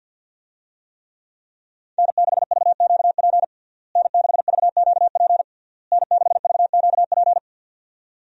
Morse code, heard as a keyed tone, about 52 wpm, 700 Hz. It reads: N6VQP N6VQP N6VQP
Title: USA Call Signs - 5 Characters - Repeated 3x - 50wpm